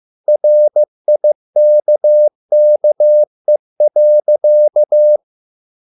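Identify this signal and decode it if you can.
Morse code, heard as a keyed tone, about 15 wpm, 600 Hz. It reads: RIKKE.
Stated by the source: Morse code at a slow 15 wpm from CW med Gnister.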